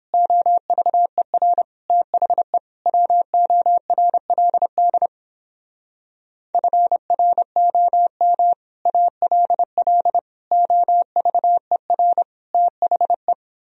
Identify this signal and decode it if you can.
Morse code, heard as a keyed tone, about 29 wpm, 700 Hz. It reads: OVER THE WORLD FROM ALL OVER THE